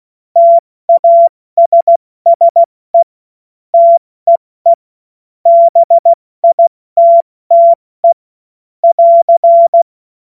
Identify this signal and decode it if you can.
Morse code, heard as a keyed tone, about 16 wpm, 680 Hz. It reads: TASSE TEE BITTE <AR>
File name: Morsekurs / Witze / Witz 23. Mai 2023 70 F80